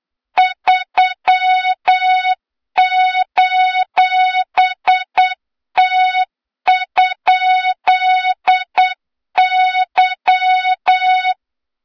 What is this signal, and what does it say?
Morse code, heard as a keyed tone, about 8 wpm, 3700 Hz, with noise. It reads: 3:T?Y
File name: Morze-HA2MN / hang / 23 GYAKORLAT